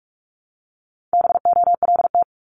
Morse code, heard as a keyed tone, about 46 wpm, 700 Hz. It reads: BOLT
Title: Multi-letter Phonemes 45wpm